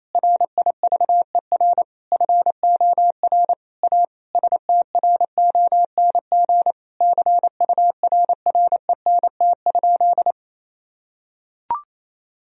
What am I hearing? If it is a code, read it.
Morse code, heard as a keyed tone, about 28 wpm, 700 Hz. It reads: RIVER FOR A STRONG CURRENT?
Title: Sentences from Top 1000 Words - Review Part 4 - 28wpm